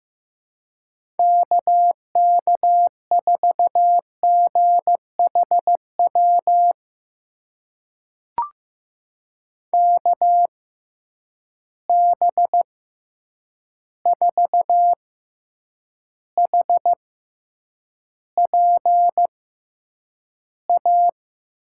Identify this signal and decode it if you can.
Morse code, heard as a keyed tone, about 15 wpm, spaced as for 8 wpm, 700 Hz. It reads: KK4GHW KB4HPA
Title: USA Call Signs - 6x Character Spacing - 15wpm